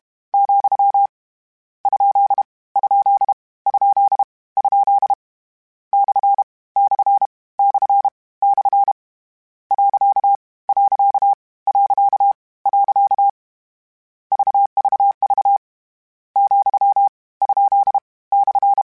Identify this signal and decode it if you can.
Morse code, heard as a keyed tone, about 32 wpm, 800 Hz. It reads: , ? ? ? ? / / / / . . VVV , ? /
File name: Misc / Misc-mp3s / Punct @30